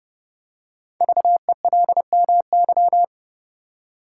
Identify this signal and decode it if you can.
Morse code, heard as a keyed tone, about 29 wpm, 690 Hz. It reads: VELMY
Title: Code Groups - 5 Letters - 30wpm